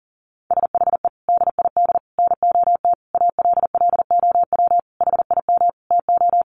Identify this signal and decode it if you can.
Morse code, heard as a keyed tone, about 40 wpm, 700 Hz. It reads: SHE DID NOT ALLOW HIM TO
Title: Sentences from Top 1200 Words 40wpm